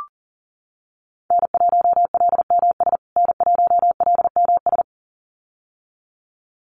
Morse code, heard as a keyed tone, about 40 wpm, 700 Hz. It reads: N1LMS N1LMS
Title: USA Call Signs Repeated 2x 40wpm